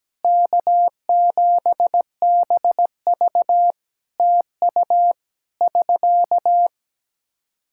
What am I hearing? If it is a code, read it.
Morse code, heard as a keyed tone, about 17 wpm, 700 Hz. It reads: K7BV TU <SK>
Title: K1USN SST QSO Practice 17wpm